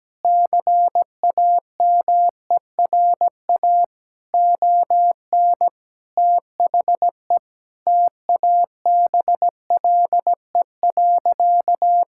Morse code, heard as a keyed tone, about 17 wpm, 700 Hz. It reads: CAMERA ON THE TABLE.
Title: Sentences from Top 1000 Words - Review Part 5 - 17wpm